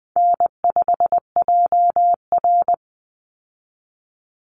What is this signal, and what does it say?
Morse code, heard as a keyed tone, about 20 wpm, 700 Hz. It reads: N5JR
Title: Easy Call Signs Repeated 2x - 20wpm